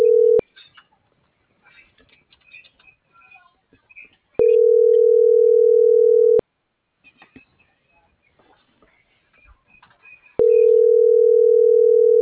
Ringback tone (North America), 440 + 480 Hz.